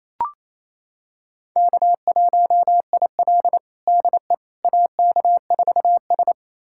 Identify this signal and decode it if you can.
Morse code, heard as a keyed tone, about 28 wpm, 700 Hz. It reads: K1IL DE AK4S
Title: USA Callsign DE Callsign - 4 Characters - 28wpm